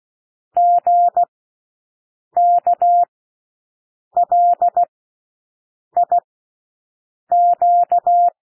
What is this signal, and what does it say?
Morse code, heard as a keyed tone, about 16 wpm, spaced as for 6 wpm, 700 Hz. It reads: GKLIQ
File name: cwmp3 / L1 SOUND LETTERS 16 wpm